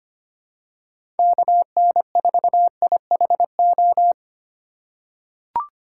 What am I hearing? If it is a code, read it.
Morse code, heard as a keyed tone, about 25 wpm, 700 Hz. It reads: KN4IHO
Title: USA Call Signs - 6 Characters - 4x Character Spacing - 25wpm